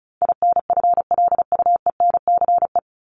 Morse code, heard as a keyed tone, about 34 wpm, 700 Hz. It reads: INFLUENCE
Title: Top 1001-1250 Words 35wpm